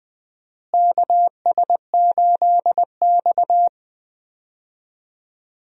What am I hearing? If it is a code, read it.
Morse code, heard as a keyed tone, about 20 wpm, 700 Hz. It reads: KS8X
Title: USA Call Signs - 4 Characters - Repeated 3x - 20wpm